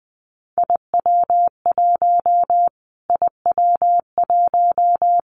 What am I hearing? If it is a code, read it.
Morse code, heard as a keyed tone, about 19 wpm, 700 Hz. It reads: IW1 IW1
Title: Callsign Prefixes for Top 25 DXCC - Repeated 3x - 20wpm